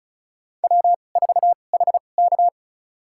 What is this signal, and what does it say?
Morse code, heard as a keyed tone, about 35 wpm, spaced as for 25 wpm, 700 Hz. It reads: W4HK
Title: USA Call Signs - 4 Characters - 2x Character Spacing - 35wpm